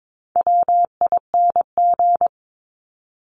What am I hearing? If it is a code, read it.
Morse code, heard as a keyed tone, about 22 wpm, 700 Hz. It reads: WING
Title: Top 1001-1250 Words 22wpm